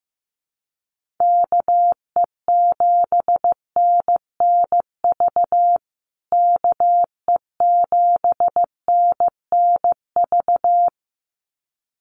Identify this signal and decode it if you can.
Morse code, heard as a keyed tone, about 14 wpm, 700 Hz. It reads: KE7NNV KE7NNV
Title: USA Call Signs Repeated 2x 15wpm